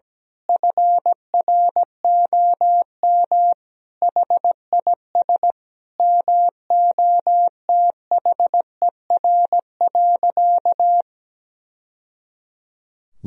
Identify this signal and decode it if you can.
Morse code, heard as a keyed tone, about 17 wpm, 700 Hz. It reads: FROM HIS MOTHER.
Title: Sentences from Top 1000 Words - Review Part 4 - 17wpm